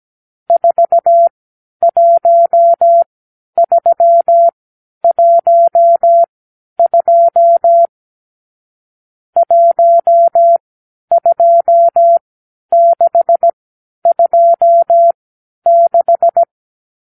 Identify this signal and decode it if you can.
Morse code, heard as a keyed tone, about 17 wpm, spaced as for 10 wpm, 680 Hz. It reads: 41312 12626
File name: k7qo / Lesson 50 (1-6)